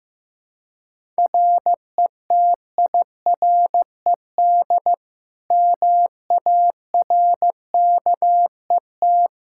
Morse code, heard as a keyed tone, about 15 wpm, 700 Hz. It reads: RETIRED MARKET
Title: Practice QSO - Exchange 3 - More Info - 15wpm